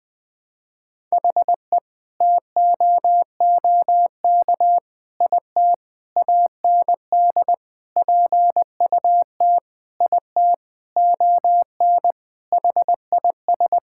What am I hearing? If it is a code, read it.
Morse code, heard as a keyed tone, about 21 wpm, 700 Hz. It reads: HE TOOK IT AND PUT IT ON HIS
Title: Sentences from Top 1600 Words 20wpm